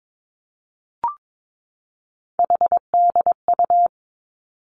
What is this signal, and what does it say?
Morse code, heard as a keyed tone, about 22 wpm, 700 Hz. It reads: HDU
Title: Three Challenging Letter-Number Combinations 22wpm